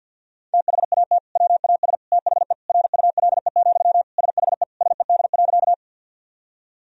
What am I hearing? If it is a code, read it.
Morse code, heard as a keyed tone, about 51 wpm, 700 Hz. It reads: THAT WAS THE RULE, SHE SED.
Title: Sentences from Top 400 Words and Abbreviations 50wpm